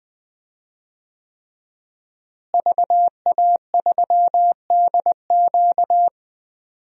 Morse code, heard as a keyed tone, about 19 wpm, 700 Hz. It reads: VA3DQ
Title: DXCC-100 Call Signs - 5 Characters - Repeated 3x - 20wpm